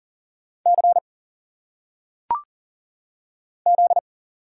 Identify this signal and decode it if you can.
Morse code, heard as a keyed tone, about 40 wpm, 700 Hz. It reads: C Z
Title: Single Letters - Phonetic Pronunciation 40wpm